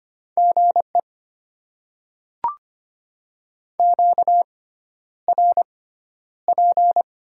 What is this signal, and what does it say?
Morse code, heard as a keyed tone, about 25 wpm, spaced as for 11 wpm, 690 Hz. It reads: GE QRP
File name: Top 90 QSO Elements - 6x Character Spacing - 25wpm